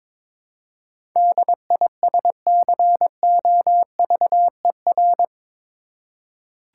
Morse code, heard as a keyed tone, about 22 wpm, 700 Hz. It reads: DISCOVER
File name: practice / 22wpm / Eight Letter Words - 22wpm